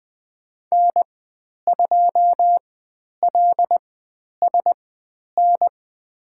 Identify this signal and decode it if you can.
Morse code, heard as a keyed tone, about 20 wpm, spaced as for 10 wpm, 700 Hz. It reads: N2LSN
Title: 102 Lesson 17g - Letter u - Call Signs 20wpm